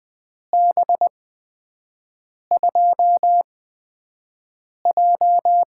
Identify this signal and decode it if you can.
Morse code, heard as a keyed tone, about 20 wpm, spaced as for 5 wpm, 700 Hz. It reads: B2J
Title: USA Call Signs - 6 Characters - 8x Character Spacing - 20wpm